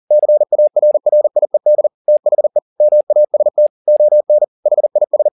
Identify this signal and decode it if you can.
Morse code, heard as a keyed tone, about 40 wpm, 600 Hz. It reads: CARRIED THE MAST ON HIS